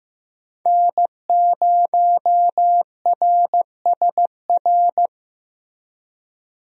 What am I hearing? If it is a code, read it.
Morse code, heard as a keyed tone, about 15 wpm, 700 Hz. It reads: N0RSR
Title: USA Call Signs - 5 Characters - Repeated 2x - 15wpm